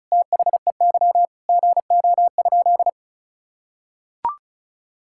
Morse code, heard as a keyed tone, about 36 wpm, 700 Hz. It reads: THEY GO?
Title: Sentences from Top 1000 Words - Review Part 5 - 35wpm